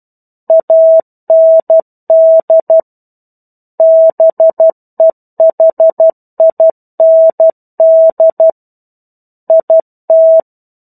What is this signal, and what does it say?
Morse code, heard as a keyed tone, about 12 wpm, 650 Hz. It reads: AND BEHIND IT
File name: Chapter-01 12 WPM